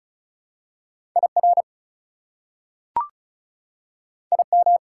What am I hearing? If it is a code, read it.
Morse code, heard as a keyed tone, about 36 wpm, 700 Hz. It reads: IR IM
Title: Prefixes and Suffixes 35wpm